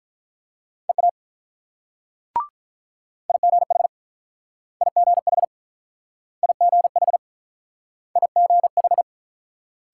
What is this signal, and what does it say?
Morse code, heard as a keyed tone, about 49 wpm, 700 Hz. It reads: EA IGH IGH IGH IGH